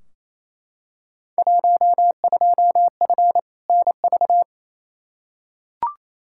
Morse code, heard as a keyed tone, about 28 wpm, 700 Hz. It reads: 12F NV
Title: Field Day Exchanges Encore - 28wpm